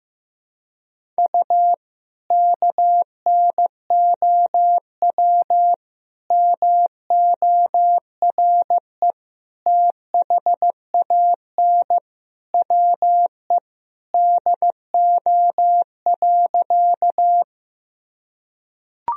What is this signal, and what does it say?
Morse code, heard as a keyed tone, about 16 wpm, 700 Hz. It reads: U KNOW MORE THAN WE DO.
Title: Sentences from Top 100 Words and Abbreviations 15wpm